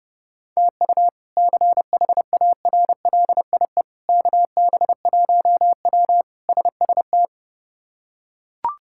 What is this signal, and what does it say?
Morse code, heard as a keyed tone, about 32 wpm, 700 Hz. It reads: TU CHARLIE KB1W SST
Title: K1USN SST QSO Practice - 30wpm